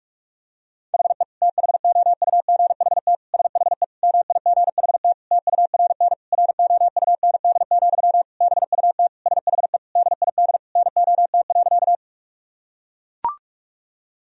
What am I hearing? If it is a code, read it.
Morse code, heard as a keyed tone, about 41 wpm, 700 Hz. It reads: HE THOUGHT SHE MIGHT TURN ROUND, BUT SHE DID NOT.
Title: Sentences from Top 1000 Words - Review Part 1 - 45wpm